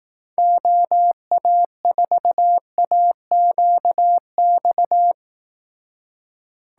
Morse code, heard as a keyed tone, about 18 wpm, 700 Hz. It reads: OA4AQX
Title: Speed Racing - USA 2x3 Call Signs 15wpm